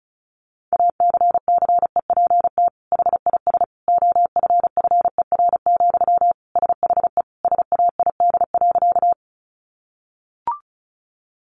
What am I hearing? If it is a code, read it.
Morse code, heard as a keyed tone, about 34 wpm, 700 Hz. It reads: ACCEPT HIS OFFER, SHE SAID.
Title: Sentences from Top 1300 Words 35wpm